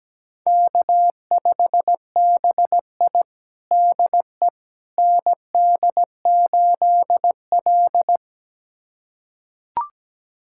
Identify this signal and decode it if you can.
Morse code, heard as a keyed tone, about 17 wpm, 700 Hz. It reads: K5BI DE ND8L
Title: USA Callsign DE Callsign - 4 Characters - 17wpm